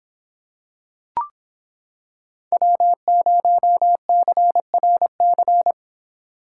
Morse code, heard as a keyed tone, about 26 wpm, 700 Hz. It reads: W0CRC